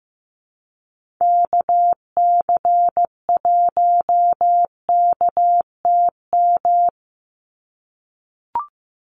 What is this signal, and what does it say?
Morse code, heard as a keyed tone, about 15 wpm, 700 Hz. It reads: KC1KTM
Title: USA Call Signs 15wpm